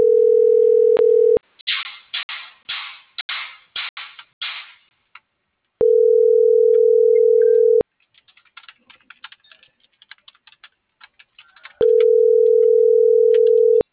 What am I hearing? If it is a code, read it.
Ringback tone (North America), 440 + 480 Hz.